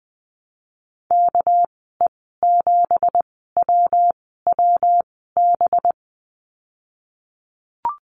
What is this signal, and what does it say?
Morse code, heard as a keyed tone, about 19 wpm, spaced as for 14 wpm, 700 Hz. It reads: KE7WWB
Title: 200 Lesson 32g - Letter k - Call Signs 20wpm